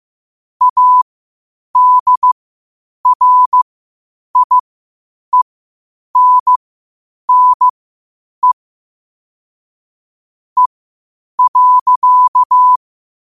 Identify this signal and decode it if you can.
Morse code, heard as a keyed tone, about 15 wpm, spaced as for 8 wpm, 1000 Hz. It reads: ADRIENNE E.
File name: qotdfiles / 8wpm